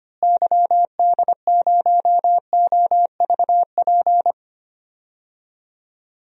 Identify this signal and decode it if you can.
Morse code, heard as a keyed tone, about 25 wpm, 700 Hz. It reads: YD0OVP
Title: DXCC-100 Call Signs - 6 Characters - Repeated 3x - 25wpm